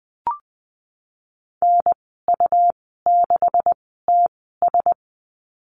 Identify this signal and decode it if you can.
Morse code, heard as a keyed tone, about 20 wpm, spaced as for 14 wpm, 700 Hz. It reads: NU6TS
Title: 128 Lesson 21g - Number 6 - Call Signs 20wpm